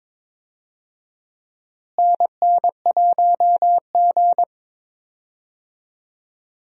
Morse code, heard as a keyed tone, about 22 wpm, 700 Hz. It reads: NN1G
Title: USA Call Signs - 4 Characters - Repeated 3x - 22wpm